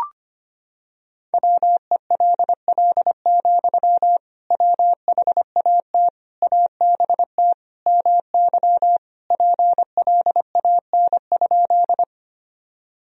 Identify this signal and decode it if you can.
Morse code, heard as a keyed tone, about 25 wpm, 700 Hz. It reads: WELL, WHAT ABT MY PLAN?
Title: Sentences from Top 500 Words and Abbreviations 25wpm